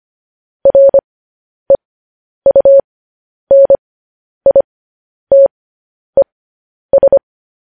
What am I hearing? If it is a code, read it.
Morse code, heard as a keyed tone, about 25 wpm, spaced as for 10 wpm, 550 Hz. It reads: REUNITES
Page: news-headlines-in-morse-code-at-10-wpm